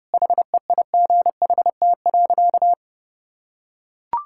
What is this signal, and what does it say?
Morse code, heard as a keyed tone, about 30 wpm, 700 Hz. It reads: HEIGHT.
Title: Sentences from Top 1600 Words 30wpm